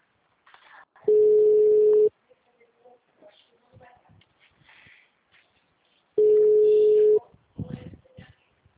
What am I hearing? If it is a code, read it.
Ringback tone (Europe, 425 Hz).